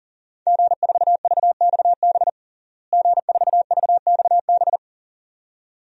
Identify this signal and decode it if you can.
Morse code, heard as a keyed tone, about 37 wpm, 700 Hz. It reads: G4VXB G4VXB
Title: DXCC-100 Call Signs - 5 Characters - Repeated 2x - 40wpm